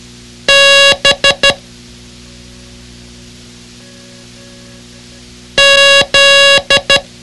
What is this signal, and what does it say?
Morse code, heard as a keyed tone, about 12 wpm, 2800 Hz. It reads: B Z
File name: ham radio / CodeQuick / Code Quick 06